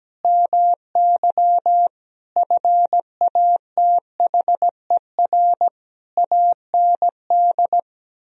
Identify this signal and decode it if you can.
Morse code, heard as a keyed tone, about 17 wpm, 700 Hz. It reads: MY FATHER AND
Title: Sentences from Top 1000 Words - Review Part 3 - 17wpm